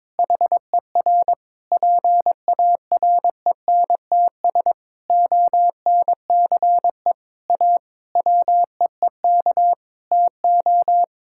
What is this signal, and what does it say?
Morse code, heard as a keyed tone, about 22 wpm, 700 Hz. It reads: HER PARENTS ONCE A WEEK TO
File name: Difficult Sentences with Hints - 22wpm